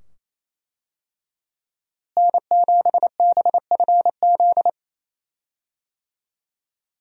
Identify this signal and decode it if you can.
Morse code, heard as a keyed tone, about 28 wpm, 700 Hz. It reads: N7BFZ
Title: USA Call Signs - 5 Characters - Repeated 2x - 28wpm